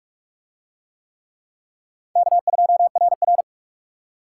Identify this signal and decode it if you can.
Morse code, heard as a keyed tone, about 43 wpm, 700 Hz. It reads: K2RR